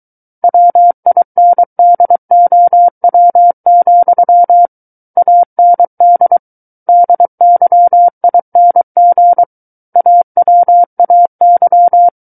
Morse code, heard as a keyed tone, about 23 wpm, 700 Hz. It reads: WINDOW, AND DYING AWAY
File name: Book One / 14-Chapter 14 - 23 WPM